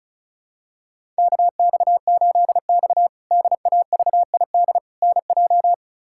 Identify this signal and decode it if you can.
Morse code, heard as a keyed tone, about 36 wpm, 700 Hz. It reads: KX8X DAVID NJ
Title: K1USN SST QSO Practice - 35wpm